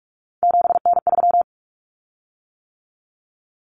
Morse code, heard as a keyed tone, about 44 wpm, 700 Hz. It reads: 7N3
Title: Common Cut Numbers - 3-Digit 45wpm